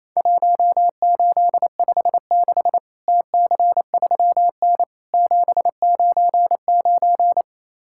Morse code, heard as a keyed tone, about 28 wpm, 700 Hz. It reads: 1856 TC3N 799
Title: Worked All Europe DX Contest - QTC Exchange - 28wpm